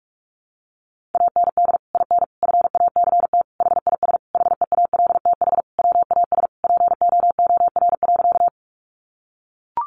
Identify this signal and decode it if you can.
Morse code, heard as a keyed tone, about 44 wpm, 700 Hz. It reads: AND IN FACT HIS HEALTH WAS POOR.
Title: Sentences from Top 1000 Words 45wpm